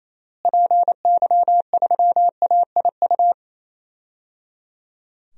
Morse code, heard as a keyed tone, about 28 wpm, 700 Hz. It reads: PY3AIU
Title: DXCC-100 Call Signs - 6 Characters - Repeated 3x - 28wpm